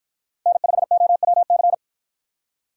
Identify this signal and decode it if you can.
Morse code, heard as a keyed tone, about 46 wpm, 700 Hz. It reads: N5OWC